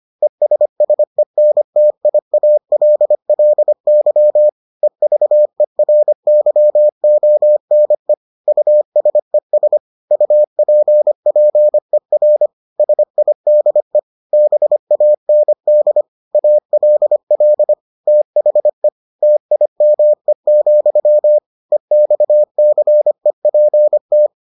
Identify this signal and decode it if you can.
Morse code, heard as a keyed tone, about 25 wpm, 600 Hz. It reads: ESSENTIALLY EVERYONE USES UPPER SIDE BAND ALL THE TIME, EXCEPT